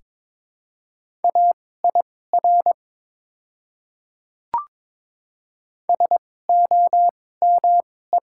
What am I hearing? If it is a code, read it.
Morse code, heard as a keyed tone, about 21 wpm, spaced as for 16 wpm, 700 Hz. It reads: AIR SOME
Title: 2x Character Spacing - Top 200 Words - 22wpm